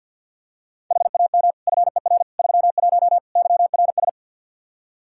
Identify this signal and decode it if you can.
Morse code, heard as a keyed tone, about 47 wpm, 700 Hz. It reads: HAM FER 31 YRS
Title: K1USN SST QSO Practice 50wpm